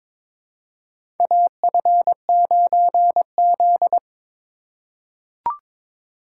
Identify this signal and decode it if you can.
Morse code, heard as a keyed tone, about 22 wpm, 700 Hz. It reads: AF9Z
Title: USA Call Signs - 4 Characters - 4x Character Spacing - 22wpm